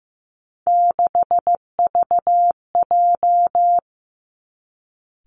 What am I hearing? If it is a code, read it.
Morse code, heard as a keyed tone, about 15 wpm, 700 Hz. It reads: 6VJ
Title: Three Challenging Letter-Number Combinations 15wpm